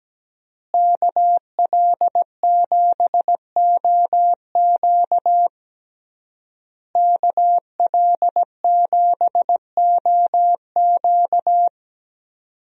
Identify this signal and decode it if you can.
Morse code, heard as a keyed tone, about 17 wpm, 700 Hz. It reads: KL7OQ KL7OQ
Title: DXCC-100 Call Signs - 5 Characters - Repeated 3x - 17wpm